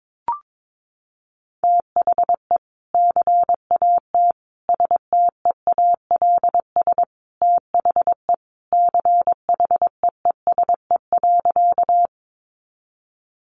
Morse code, heard as a keyed tone, about 22 wpm, 700 Hz. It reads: THE CAT STEALS THE CHEESE.
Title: Easy Phrases 22wpm